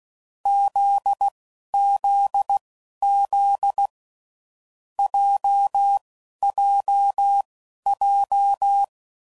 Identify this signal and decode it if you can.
Morse code, heard as a keyed tone, about 16 wpm, spaced as for 12 wpm, 800 Hz. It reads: ZZZ JJJ
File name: Groups / Basic Letter Groups / JZ17